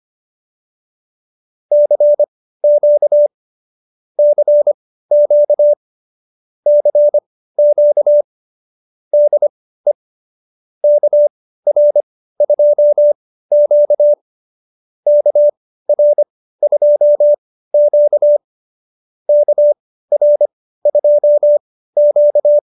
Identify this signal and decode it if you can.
Morse code, heard as a keyed tone, about 25 wpm, spaced as for 15 wpm, 590 Hz. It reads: CQ CQ CQ DE KR2Q KR2Q KR2Q